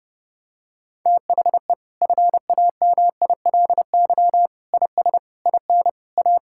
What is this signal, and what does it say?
Morse code, heard as a keyed tone, about 30 wpm, 700 Hz. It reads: THE FAMILY IS IN A
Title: Sentences from Top 1000 Words - Review Part 5 - 30wpm